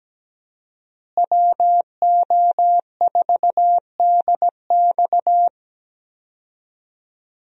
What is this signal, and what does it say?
Morse code, heard as a keyed tone, about 17 wpm, 700 Hz. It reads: WO4DX
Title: USA Call Signs - 5 Characters - Repeated 2x - 17wpm